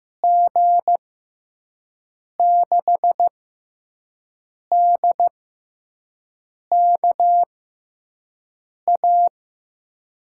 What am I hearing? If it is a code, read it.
Morse code, heard as a keyed tone, about 16 wpm, spaced as for 5 wpm, 700 Hz. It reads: G6DKA
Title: USA Call Signs - 6 Characters - 6x Character Spacing - 15wpm